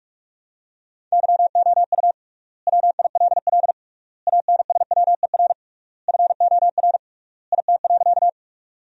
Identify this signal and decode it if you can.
Morse code, heard as a keyed tone, about 46 wpm, 700 Hz. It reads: YOU WILL ANSWER FOR IT.
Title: 3x Word-Spacing - Sentences from Top 300 Words - 45wpm